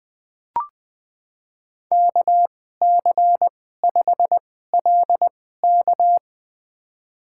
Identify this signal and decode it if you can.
Morse code, heard as a keyed tone, about 20 wpm, spaced as for 14 wpm, 700 Hz. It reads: KC5LK
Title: USA Call Signs - 2x Character Spacing - 20wpm